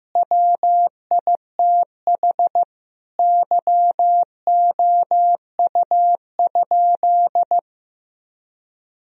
Morse code, heard as a keyed tone, about 16 wpm, 700 Hz. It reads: WITH YOU?
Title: Sentences from Top 1000 Words - Review Part 1 - 15wpm